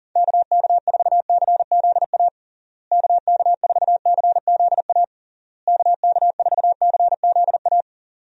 Morse code, heard as a keyed tone, about 37 wpm, 700 Hz. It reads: KK4CZA KK4CZA KK4CZA
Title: USA Call Signs - 6 Characters - Repeated 3x - 40wpm